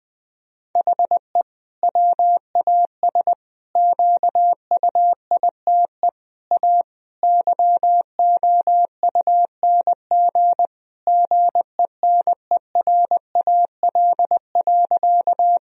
Morse code, heard as a keyed tone, about 20 wpm, 700 Hz. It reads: HE WAS QUITE A YOUNG GENERAL.
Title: Sentences from Top 1000 Words - Review Part 3 - 20wpm